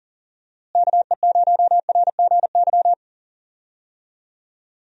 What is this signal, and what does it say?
Morse code, heard as a keyed tone, about 40 wpm, 700 Hz. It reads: KE0RGY
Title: USA Call Signs - 6 Characters - Repeated 2x - 40wpm